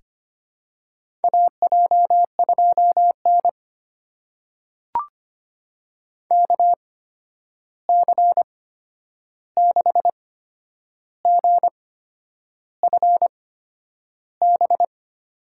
Morse code, heard as a keyed tone, about 25 wpm, spaced as for 9 wpm, 700 Hz. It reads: AJ2N KC6GFB
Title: USA Call Signs - 8x Character Spacing - 25wpm